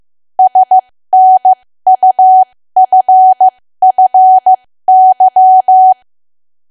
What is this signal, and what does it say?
Morse code, heard as a keyed tone, about 15 wpm, spaced as for 13 wpm, 750 Hz. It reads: SNUFFY
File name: cwa / W1AW-Level2 / 13WPM